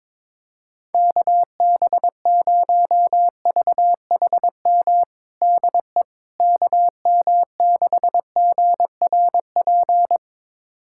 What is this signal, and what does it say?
Morse code, heard as a keyed tone, about 22 wpm, 700 Hz. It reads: KB0VHM DE KM6GRP
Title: USA Callsign DE Callsign - 6 Characters - 22wpm